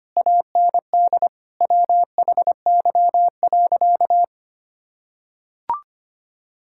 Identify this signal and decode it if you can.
Morse code, heard as a keyed tone, about 25 wpm, 700 Hz. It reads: AND WHY.
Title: Sentences from Top 1600 Words 25wpm